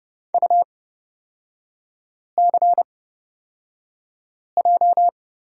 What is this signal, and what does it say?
Morse code, heard as a keyed tone, about 31 wpm, spaced as for 5 wpm, 700 Hz. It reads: UCJ
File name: Lesson 36e - BT - Three Characters 30wpm